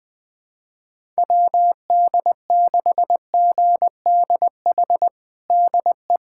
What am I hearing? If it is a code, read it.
Morse code, heard as a keyed tone, about 20 wpm, 700 Hz. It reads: WD6GDH DE
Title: USA Callsign DE Callsign - 6 Characters - 20wpm